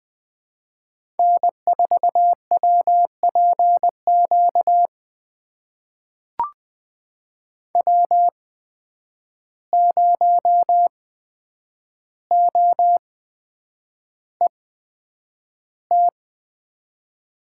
Morse code, heard as a keyed tone, about 20 wpm, spaced as for 9 wpm, 700 Hz. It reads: N4WPQ W0OET
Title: USA Call Signs - 5 Characters - 8x Character Spacing - 20wpm